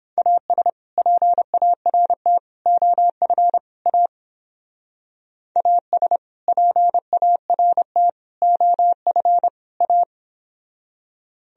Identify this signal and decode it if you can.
Morse code, heard as a keyed tone, about 29 wpm, 700 Hz. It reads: AS PART OF A AS PART OF A